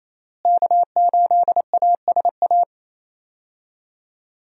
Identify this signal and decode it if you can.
Morse code, heard as a keyed tone, about 28 wpm, 700 Hz. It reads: K8ASA
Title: USA Call Signs - 5 Characters - Repeated 3x - 28wpm